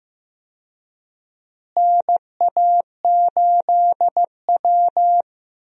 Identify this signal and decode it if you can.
Morse code, heard as a keyed tone, about 15 wpm, 700 Hz. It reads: NA8W
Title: USA Call Signs - 4 Characters - Repeated 3x - 15wpm